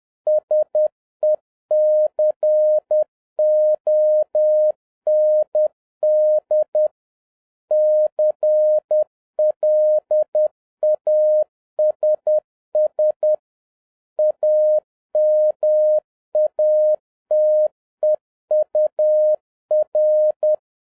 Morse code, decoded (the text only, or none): SECOND CLASS AMATEUR